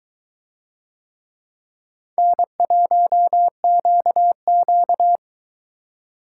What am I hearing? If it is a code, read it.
Morse code, heard as a keyed tone, about 22 wpm, 700 Hz. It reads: N1QQ